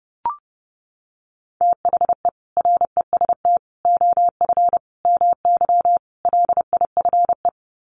Morse code, heard as a keyed tone, about 30 wpm, 700 Hz. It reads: THE REST OF MY LIFE
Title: Sets of 5 Words 30wpm